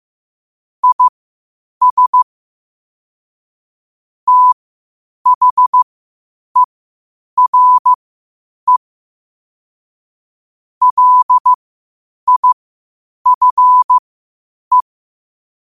Morse code, heard as a keyed tone, about 15 wpm, spaced as for 8 wpm, 1000 Hz. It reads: IS THERE LIFE
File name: qotdfiles / 8wpm